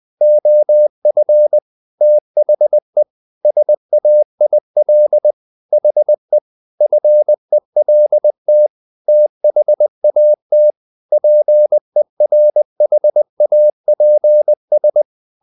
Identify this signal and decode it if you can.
Morse code, heard as a keyed tone, about 20 wpm, 600 Hz. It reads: OF THE SAIL HE FELT THAT PERHAPS
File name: the old man and the sea part 2 20wpm